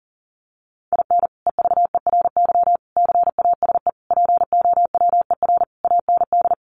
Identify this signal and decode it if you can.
Morse code, heard as a keyed tone, about 40 wpm, 700 Hz. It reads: IN EVERY CASE POWER AND
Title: Sentences from Top 1200 Words 40wpm